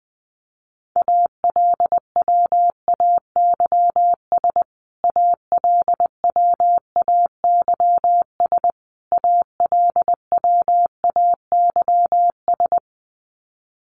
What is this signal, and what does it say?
Morse code, heard as a keyed tone, about 20 wpm, 700 Hz. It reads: ALWAYS ALWAYS ALWAYS
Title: Top 300 Words Repeated 20wpm